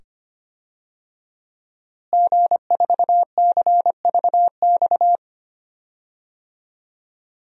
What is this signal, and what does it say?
Morse code, heard as a keyed tone, about 25 wpm, 700 Hz. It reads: G4CVX
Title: DXCC-100 Call Signs - 5 Characters - Repeated 2x - 25wpm